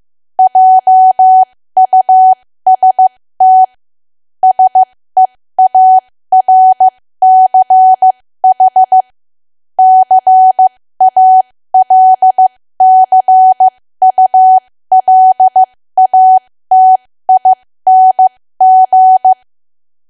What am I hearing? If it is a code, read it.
Morse code, heard as a keyed tone, about 15 wpm, spaced as for 13 wpm, 750 Hz. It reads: JUST SEARCH CALCULATING
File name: cwa / W1AW-Level2 / 13WPM